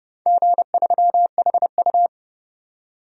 Morse code, heard as a keyed tone, about 31 wpm, 700 Hz. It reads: G3HU
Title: DXCC-100 Call Signs - 4 Characters - 30wpm